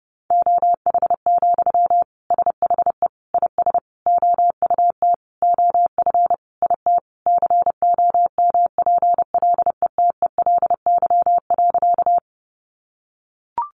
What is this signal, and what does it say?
Morse code, heard as a keyed tone, about 30 wpm, 700 Hz. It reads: OH, SHE IS OUT OF IT COMPLETELY.
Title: Sentences from Top 1100 Words 30wpm